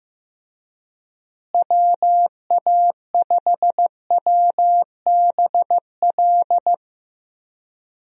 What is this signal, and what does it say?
Morse code, heard as a keyed tone, about 15 wpm, 700 Hz. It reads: WA5WBL
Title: USA Call Signs - 6 Characters - Repeated 3x - 15wpm